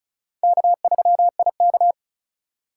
Morse code, heard as a keyed tone, about 36 wpm, 690 Hz. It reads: K3IK
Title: USA Call Signs - 4 Characters - 2x Character Spacing - 35wpm